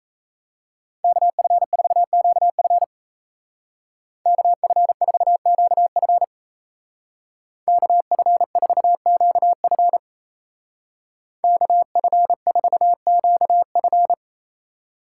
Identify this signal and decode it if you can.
Morse code, heard as a keyed tone, about 37 wpm, 700 Hz. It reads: KF4QF KF4QF KF4QF KF4QF